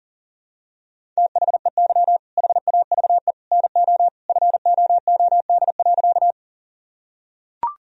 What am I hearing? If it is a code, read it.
Morse code, heard as a keyed tone, about 40 wpm, 700 Hz. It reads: THEY HAVE NO FOOD.
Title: Sentences from Top 1000 Words - Review Part 1 - 40wpm